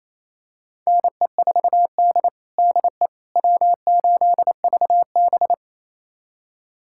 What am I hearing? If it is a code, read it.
Morse code, heard as a keyed tone, about 28 wpm, 700 Hz. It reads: NE4D DE W8VB
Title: USA Callsign DE Callsign - 4 Characters - 28wpm